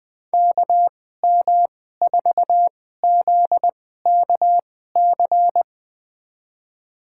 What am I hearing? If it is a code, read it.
Morse code, heard as a keyed tone, about 21 wpm, spaced as for 15 wpm, 700 Hz. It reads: KM4ZKC
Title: USA Call Signs - 2x Character Spacing - 20wpm